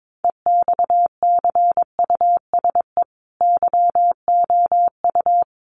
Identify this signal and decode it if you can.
Morse code, heard as a keyed tone, about 22 wpm, 700 Hz. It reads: EXCUSE YOU
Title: Sentences from Top 700 Words 22wpm